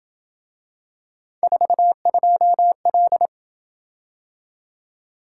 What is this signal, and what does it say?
Morse code, heard as a keyed tone, about 25 wpm, 700 Hz. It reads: 42L